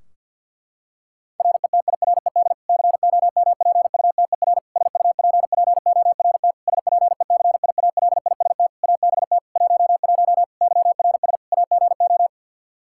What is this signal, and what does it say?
Morse code, heard as a keyed tone, about 50 wpm, 700 Hz. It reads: RETIRED COMPUTER SUPPORT SPECIALIST ABT 11 YRS AGO
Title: Practice QSO - Exchange 3 - More Info - 50wpm